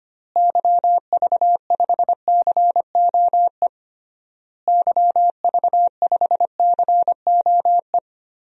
Morse code, heard as a keyed tone, about 25 wpm, 700 Hz. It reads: YV5COE YV5COE
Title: DXCC-100 Call Signs - 6 Characters - Repeated 3x - 25wpm